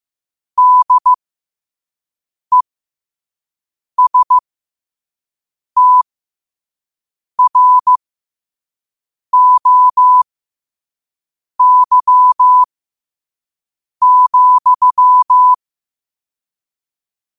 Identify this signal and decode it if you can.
Morse code, heard as a keyed tone, about 15 wpm, spaced as for 5 wpm, 1000 Hz. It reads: DESTROY,
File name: qotdfiles / 5wpm